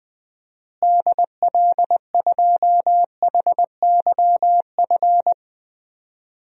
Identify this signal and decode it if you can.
Morse code, heard as a keyed tone, about 21 wpm, 700 Hz. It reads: DL2HYF
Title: DXCC-100 Call Signs - 6 Characters - Repeated 2x - 20wpm